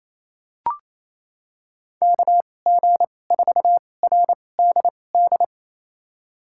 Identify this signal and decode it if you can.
Morse code, heard as a keyed tone, about 28 wpm, spaced as for 20 wpm, 700 Hz. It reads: KG4RDD
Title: USA Call Signs - 6 Characters - 2x Character Spacing - 28wpm